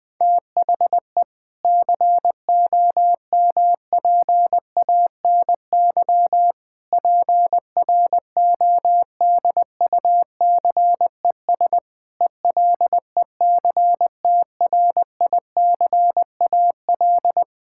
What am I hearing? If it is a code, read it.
Morse code, heard as a keyed tone, about 20 wpm, 700 Hz. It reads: THE COMPANY PRODUCES ELECTRICAL
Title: Sentences from Top 1000 Words - Review Part 6 - 20wpm